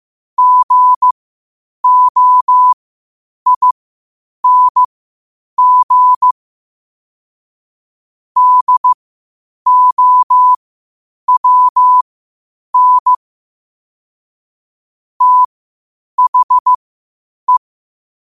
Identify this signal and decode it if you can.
Morse code, heard as a keyed tone, about 15 wpm, spaced as for 8 wpm, 1000 Hz. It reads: GOING DOWN THE
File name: 8wpm